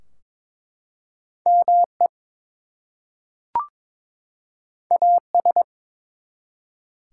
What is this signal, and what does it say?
Morse code, heard as a keyed tone, about 22 wpm, 700 Hz. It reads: ME AS
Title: Two Letter Words - 22wpm